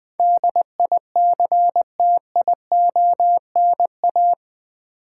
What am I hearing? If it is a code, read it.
Morse code, heard as a keyed tone, about 20 wpm, 700 Hz. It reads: DICTIONA
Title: Build A Word - 4 Letters At A Time - 10-Letter Words - 20wpm